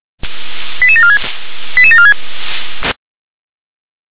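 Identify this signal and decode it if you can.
Five-tone selective call (ZVEI), heard as quick runs of five tones, tones of about 70 ms. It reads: 88746 88746
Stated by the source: ZVEI five-tone call 88702 received via SDR.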